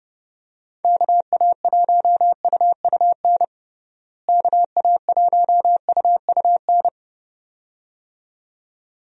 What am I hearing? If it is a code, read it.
Morse code, heard as a keyed tone, about 30 wpm, 700 Hz. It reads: KA1UUN KA1UUN
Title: USA Call Signs - 6 Characters - Repeated 2x - 30wpm